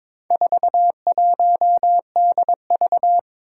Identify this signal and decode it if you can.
Morse code, heard as a keyed tone, about 22 wpm, 700 Hz. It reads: 41DV
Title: Four Challenging Letter-Number Combinations 22wpm